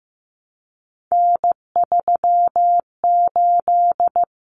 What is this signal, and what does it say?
Morse code, heard as a keyed tone, about 15 wpm, 700 Hz. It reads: N38
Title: Common Cut Numbers - 3-Digit 15wpm